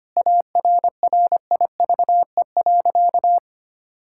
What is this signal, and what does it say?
Morse code, heard as a keyed tone, about 25 wpm, 700 Hz. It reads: ARRIVE.
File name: practice / 25wpm / Sentences with 3-Word Phrases - 25wpm